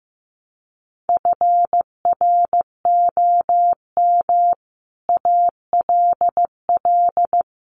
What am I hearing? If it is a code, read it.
Morse code, heard as a keyed tone, about 15 wpm, 700 Hz. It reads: FROM ALL
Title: Sets of 5 Words 15wpm